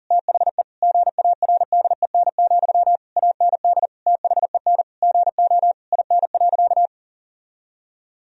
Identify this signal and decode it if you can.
Morse code, heard as a keyed tone, about 40 wpm, 700 Hz. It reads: THE GARDEN, AND THEN GO IN.
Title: Sentences from Top 1000 Words - Review Part 4 - 40wpm